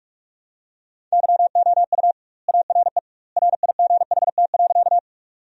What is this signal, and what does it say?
Morse code, heard as a keyed tone, about 46 wpm, 700 Hz. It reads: YOU ARE RIGHT.
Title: 2x Word-Spacing - Sentences from Top 200 Words - 45wpm